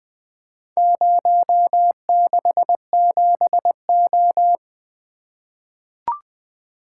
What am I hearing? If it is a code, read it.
Morse code, heard as a keyed tone, about 20 wpm, 700 Hz. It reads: 067O
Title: Four Letter-Number Combinations - 20wpm